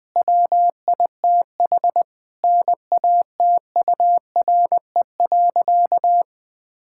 Morse code, heard as a keyed tone, about 20 wpm, 700 Hz. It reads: WITH NATURE.
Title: Sentences from Top 1000 Words - Review Part 6 - 20wpm